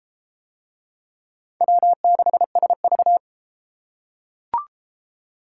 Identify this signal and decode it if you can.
Morse code, heard as a keyed tone, about 30 wpm, 700 Hz. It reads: W6SV